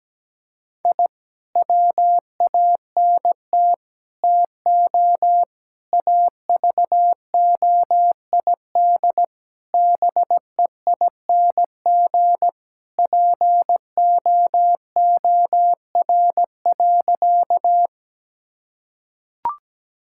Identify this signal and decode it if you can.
Morse code, heard as a keyed tone, about 17 wpm, 700 Hz. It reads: I WANT TO AVOID BEING POOR.
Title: Sentences from Top 1000 Words - Review Part 6 - 17wpm